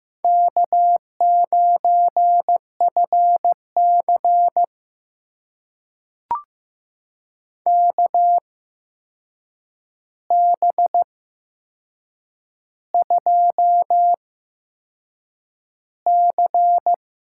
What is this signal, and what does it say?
Morse code, heard as a keyed tone, about 15 wpm, spaced as for 6 wpm, 700 Hz. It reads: K9FC KB2C
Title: USA Call Signs - 8x Character Spacing - 15wpm